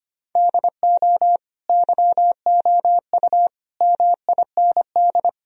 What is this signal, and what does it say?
Morse code, heard as a keyed tone, about 25 wpm, 700 Hz. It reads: DO YOU MIND
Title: Sentences from Top 1000 Words - Review Part 2 - 25wpm